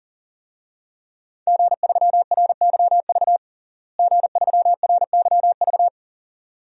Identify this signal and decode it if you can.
Morse code, heard as a keyed tone, about 40 wpm, 700 Hz. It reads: G3RYV G3RYV
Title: DXCC-100 Call Signs - 5 Characters - Repeated 2x - 40wpm